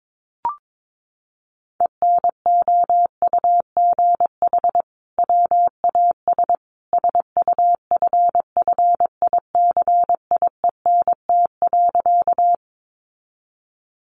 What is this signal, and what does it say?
Morse code, heard as a keyed tone, about 22 wpm, 700 Hz. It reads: ENOUGH WAS SUFFICIENT.
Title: Sentences from Top 1400 Words 22wpm